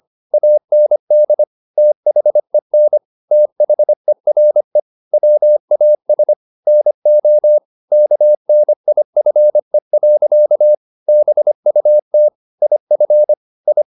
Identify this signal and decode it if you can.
Morse code, heard as a keyed tone, about 25 wpm, 600 Hz. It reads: AND THEN THERE WAS NO KNIFE. BUT IF I